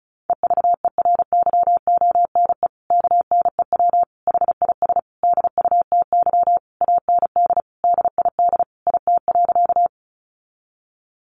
Morse code, heard as a keyed tone, about 34 wpm, 700 Hz. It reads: EVERYONE KNEW HIS DUTY AND DID IT.
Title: Sentences from Top 1500 Words 35wpm